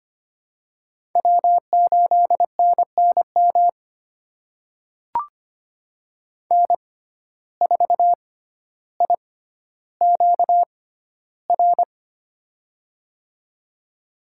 Morse code, heard as a keyed tone, about 25 wpm, spaced as for 13 wpm, 700 Hz. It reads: W8NNM N4IQR
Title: USA Call Signs - 5 Characters - 6x Character Spacing - 25wpm